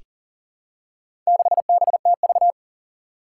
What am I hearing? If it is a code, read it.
Morse code, heard as a keyed tone, about 42 wpm, 700 Hz. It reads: 6BTV
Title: Four Challenging Letter-Number Combinations 40wpm